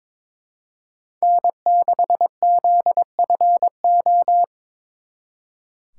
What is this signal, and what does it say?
Morse code, heard as a keyed tone, about 26 wpm, 700 Hz. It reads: N6ZFO
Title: Speed Racing - USA 1x3 Call Signs 22wpm